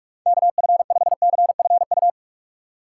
Morse code, heard as a keyed tone, about 46 wpm, 700 Hz. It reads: KF5CFU
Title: USA Call Signs - 6 Characters - 4x Character Spacing - 45wpm